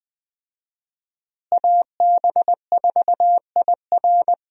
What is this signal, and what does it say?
Morse code, heard as a keyed tone, about 20 wpm, 700 Hz. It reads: AB4IR
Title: USA Call Signs - 5 Characters - Repeated 2x - 20wpm